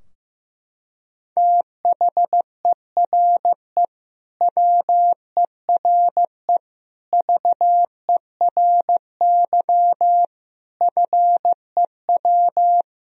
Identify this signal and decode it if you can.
Morse code, heard as a keyed tone, about 15 wpm, 700 Hz. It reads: THERE WERE VERY FEW
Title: Sentences from Top 1000 Words - Review Part 2 - 15wpm